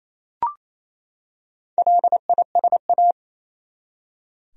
Morse code, heard as a keyed tone, about 27 wpm, 700 Hz. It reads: LISA
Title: USA Common Names - Max 4 Letters - 28wpm